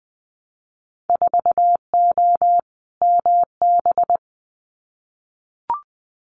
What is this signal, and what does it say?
Morse code, heard as a keyed tone, about 20 wpm, 700 Hz. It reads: 4O MB
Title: Winter Field Day Exchanges Encore 20wpm